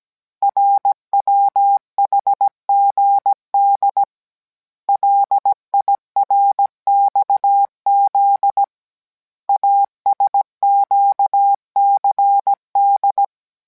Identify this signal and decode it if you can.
Morse code, heard as a keyed tone, about 17 wpm, 800 Hz. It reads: RWHGD LIRXZ ASQCD